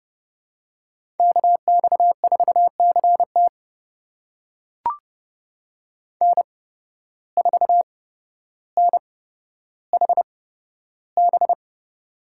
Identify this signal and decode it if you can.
Morse code, heard as a keyed tone, about 31 wpm, spaced as for 13 wpm, 700 Hz. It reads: KX4CT N4NHB
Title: USA Call Signs - 5 Characters - 8x Character Spacing - 30wpm